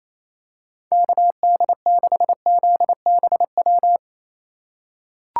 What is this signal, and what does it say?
Morse code, heard as a keyed tone, about 28 wpm, 700 Hz. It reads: KD6ZBW
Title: USA Call Signs - 6 Characters - 3x Character Spacing - 28wpm